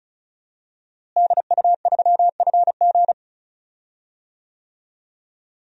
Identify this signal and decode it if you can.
Morse code, heard as a keyed tone, about 36 wpm, 700 Hz. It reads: DU3FG
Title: DXCC-100 Call Signs - 5 Characters - Repeated 2x - 35wpm